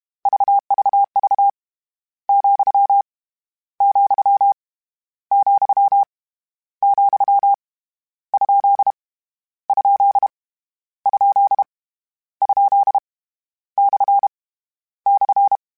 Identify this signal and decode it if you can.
Morse code, heard as a keyed tone, about 32 wpm, 800 Hz. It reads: VVV , , , , ? ? ? ? / /
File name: Misc / Misc-mp3s / Punct @30